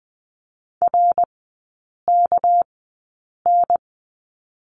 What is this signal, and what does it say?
Morse code, heard as a keyed tone, about 20 wpm, spaced as for 8 wpm, 700 Hz. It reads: RKN
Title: 198 Lesson 32e - Letter k - Three Characters 20wpm